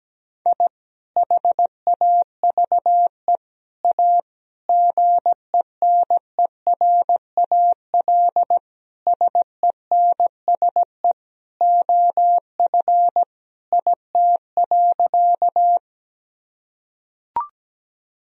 Morse code, heard as a keyed tone, about 17 wpm, 700 Hz. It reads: I HAVE A GENERAL SENSE OF IT.
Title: Sentences from Top 1000 Words - Review Part 4 - 17wpm